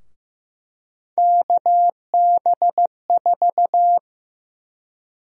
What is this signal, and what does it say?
Morse code, heard as a keyed tone, about 15 wpm, 700 Hz. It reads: KB4
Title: USA Call Sign Prefixes - 15wpm